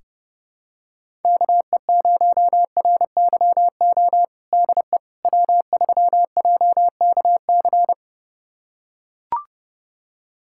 Morse code, heard as a keyed tone, about 30 wpm, 700 Hz. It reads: KE0RYO DE W3JKC
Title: USA Callsign DE Callsign 30wpm